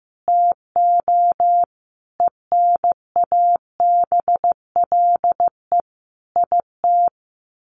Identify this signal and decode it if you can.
Morse code, heard as a keyed tone, about 15 wpm, 700 Hz. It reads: TO ENABLE IT
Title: Sentences from Top 1500 Words 15wpm